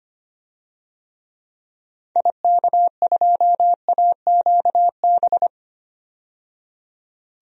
Morse code, heard as a keyed tone, about 25 wpm, 700 Hz. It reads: IK2AQB
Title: DXCC-100 Call Signs - 6 Characters - 25wpm